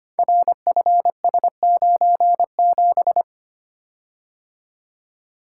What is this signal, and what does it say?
Morse code, heard as a keyed tone, about 25 wpm, 700 Hz. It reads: RFS97
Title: Code Groups - 5 Letter:Numbers - 25wpm